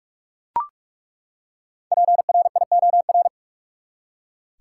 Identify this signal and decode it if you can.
Morse code, heard as a keyed tone, about 42 wpm, 700 Hz. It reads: PRIOR
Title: Five Letter Words - 45wpm